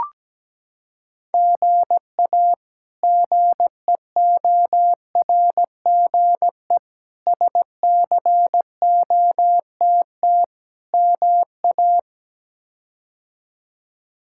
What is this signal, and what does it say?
Morse code, heard as a keyed tone, about 17 wpm, 700 Hz. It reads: GA GEORGE SCOTT MA
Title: K1USN SST QSO Practice - 17wpm